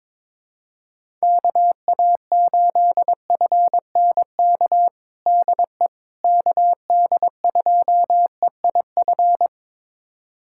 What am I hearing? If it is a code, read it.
Morse code, heard as a keyed tone, about 22 wpm, 700 Hz. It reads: KA8FNK DE KD2EIF
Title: USA Callsign DE Callsign - 6 Characters - 22wpm